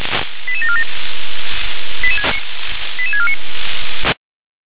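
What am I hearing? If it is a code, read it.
Five-tone selective call (ZVEI), heard as quick runs of five tones, tones of about 70 ms. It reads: 88747 88649
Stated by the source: ZVEI five-tone selective call 88766 received on an SDR.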